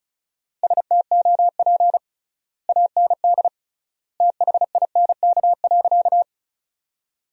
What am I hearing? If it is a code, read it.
Morse code, heard as a keyed tone, about 36 wpm, 700 Hz. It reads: STOP AND THINK.
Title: 3x Word-Spacing - Sentences from Top 300 Words - 35wpm